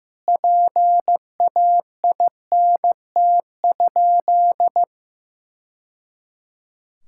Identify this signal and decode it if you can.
Morse code, heard as a keyed tone, about 15 wpm, 700 Hz. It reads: PAINT?
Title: Sentences from Top 1000 Words - Review Part 6 - 15wpm